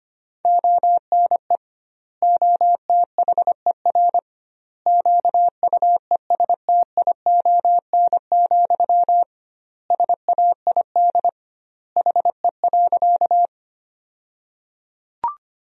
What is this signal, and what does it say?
Morse code, heard as a keyed tone, about 25 wpm, 700 Hz. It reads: ONE OTHER QUESTION, SAID HE.
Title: 2x Word-Spacing - Sentences from Top 400 Words - 25wpm